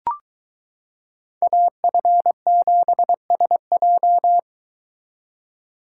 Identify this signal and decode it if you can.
Morse code, heard as a keyed tone, about 23 wpm, 700 Hz. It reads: AF7SJ